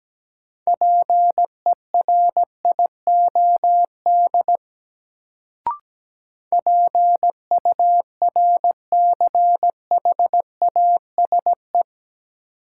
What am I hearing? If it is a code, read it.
Morse code, heard as a keyed tone, about 17 wpm, 700 Hz. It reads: PERIOD PURCHASE
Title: Sentences from Top 800 Words - New Word Drill - 17wpm